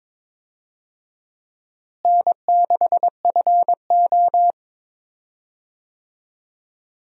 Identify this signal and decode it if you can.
Morse code, heard as a keyed tone, about 22 wpm, 700 Hz. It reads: N6FO
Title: USA Call Signs - 4 Characters - Repeated 2x - 22wpm